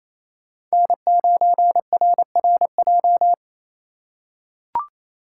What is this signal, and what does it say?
Morse code, heard as a keyed tone, about 28 wpm, 700 Hz. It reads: N9RRJ
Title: USA Call Signs - 3x Character Spacing - 28wpm